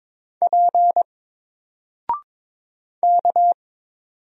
Morse code, heard as a keyed tone, about 22 wpm, 700 Hz. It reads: P K
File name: Single-Letter-Number-22wpm